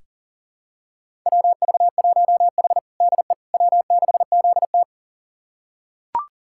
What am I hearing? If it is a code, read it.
Morse code, heard as a keyed tone, about 40 wpm, 700 Hz. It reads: WV1H DE W6ZT
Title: USA Callsign DE Callsign - 4 Characters - 40wpm